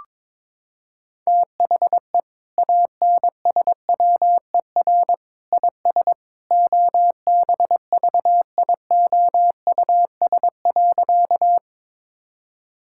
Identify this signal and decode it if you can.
Morse code, heard as a keyed tone, about 22 wpm, 700 Hz. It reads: THE ANSWER IS OBVIOUS.
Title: Sentences from Top 1600 Words 22wpm